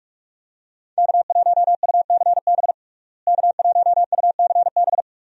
Morse code, heard as a keyed tone, about 46 wpm, 700 Hz. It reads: K1UCB K1UCB
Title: USA Call Signs - 5 Characters - Repeated 2x - 45wpm